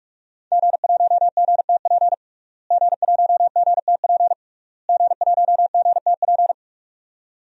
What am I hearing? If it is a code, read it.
Morse code, heard as a keyed tone, about 46 wpm, 700 Hz. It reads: G1GTP G1GTP G1GTP
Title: DXCC-100 Call Signs - 5 Characters - Repeated 3x - 45wpm